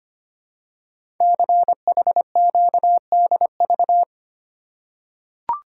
Morse code, heard as a keyed tone, about 25 wpm, 700 Hz. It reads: CHQDV
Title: Code Groups - 5 Letter:Numbers - 25wpm